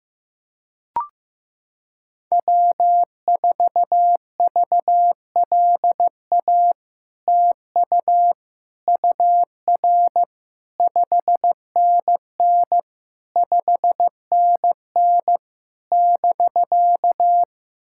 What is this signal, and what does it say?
Morse code, heard as a keyed tone, about 15 wpm, 700 Hz. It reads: W4VLA TU UR 5NN 5NN <BK>
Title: SOTA QSO Practice 15wpm